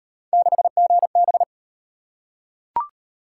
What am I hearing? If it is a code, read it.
Morse code, heard as a keyed tone, about 34 wpm, 700 Hz. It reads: 6GB